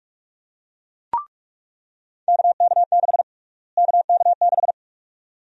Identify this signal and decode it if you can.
Morse code, heard as a keyed tone, about 46 wpm, 700 Hz. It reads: KK6 KK6
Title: USA Call Sign Prefixes - Repeated 2x - 45wpm